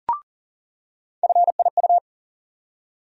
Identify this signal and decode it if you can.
Morse code, heard as a keyed tone, about 40 wpm, 700 Hz. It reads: FIU
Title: USA Call Sign Postfixes - 3 Letters 40wpm